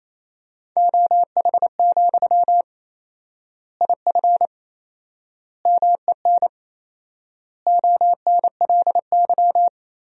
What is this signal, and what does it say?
Morse code, heard as a keyed tone, about 28 wpm, 700 Hz. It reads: OH, IF MEN ONLY
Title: 4x Word-Spacing - Sentences from Top 500 Words - 28wpm